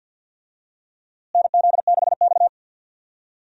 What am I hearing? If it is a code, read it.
Morse code, heard as a keyed tone, about 46 wpm, 700 Hz. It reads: NZ6X
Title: USA Call Signs - 4 Characters - Repeated 2x - 50wpm